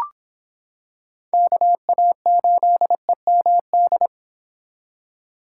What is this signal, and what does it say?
Morse code, heard as a keyed tone, about 26 wpm, 700 Hz. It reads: KA8EMD